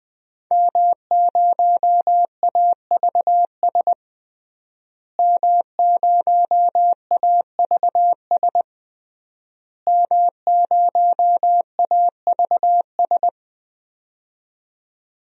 Morse code, heard as a keyed tone, about 21 wpm, 700 Hz. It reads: M0AVS M0AVS M0AVS
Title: DXCC-100 Call Signs - 5 Characters - Repeated 3x - 20wpm